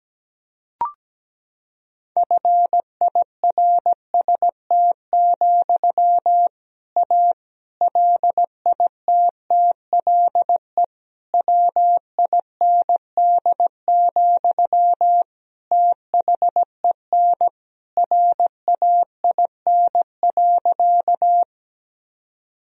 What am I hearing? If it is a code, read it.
Morse code, heard as a keyed tone, about 17 wpm, 700 Hz. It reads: FIRST, A LITTLE WIND, THEN RAIN.
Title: Sentences from Top 1000 Words - Review Part 2 - 17wpm